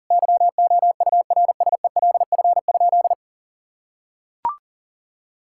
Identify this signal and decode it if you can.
Morse code, heard as a keyed tone, about 40 wpm, 700 Hz. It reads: YOURSELF?
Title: Sentences from Top 1600 Words 40wpm